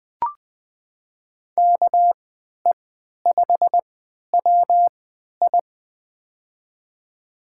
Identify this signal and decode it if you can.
Morse code, heard as a keyed tone, about 20 wpm, spaced as for 11 wpm, 700 Hz. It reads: KE5WI
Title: USA Call Signs - 5 Characters - 3x Character Spacing - 20wpm